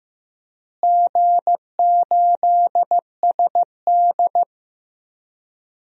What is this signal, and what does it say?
Morse code, heard as a keyed tone, about 16 wpm, 700 Hz. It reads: G8SD
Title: DXCC-100 Call Signs - 4 Characters - Repeated 2x - 15wpm